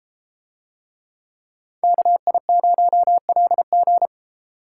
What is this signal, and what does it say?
Morse code, heard as a keyed tone, about 32 wpm, 700 Hz. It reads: KI0LG